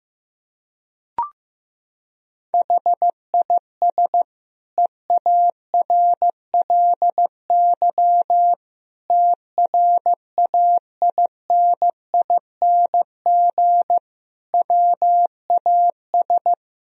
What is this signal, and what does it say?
Morse code, heard as a keyed tone, about 15 wpm, 700 Hz. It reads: HIS EARLY TRAINING WAS
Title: Sentences from Top 1000 Words - Review Part 4 - 15wpm